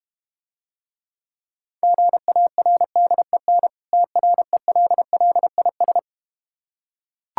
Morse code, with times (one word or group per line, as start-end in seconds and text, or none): 1.8-3.7 GARDEN
3.9-6.0 TRELLIS